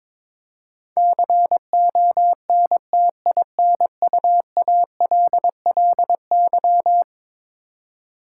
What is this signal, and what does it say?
Morse code, heard as a keyed tone, about 22 wpm, 700 Hz. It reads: CONTINUALLY
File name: practice / 22wpm / Eleven Letter Words - 22wpm